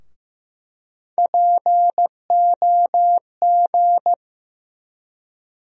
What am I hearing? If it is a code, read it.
Morse code, heard as a keyed tone, about 16 wpm, 700 Hz. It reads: POG
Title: USA Call Sign Postfixes - 3 Letters 15wpm